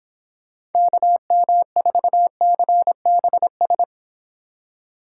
Morse code, heard as a keyed tone, about 29 wpm, 700 Hz. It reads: KM4CBS